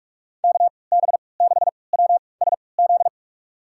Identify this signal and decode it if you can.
Morse code, heard as a keyed tone, about 46 wpm, spaced as for 26 wpm, 700 Hz. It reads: KB6WSZ
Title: USA Call Signs - 6 Characters - 3x Character Spacing - 45wpm